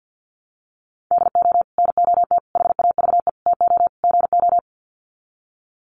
Morse code, heard as a keyed tone, about 48 wpm, 700 Hz. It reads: DO NOT HAVE TO GO
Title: Sets of 5 Words 50wpm